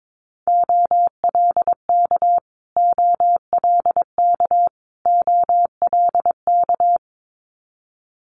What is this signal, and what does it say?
Morse code, heard as a keyed tone, about 22 wpm, 700 Hz. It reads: OLK OLK OLK
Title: Multi-letter Phonemes Repeated 3x 22wpm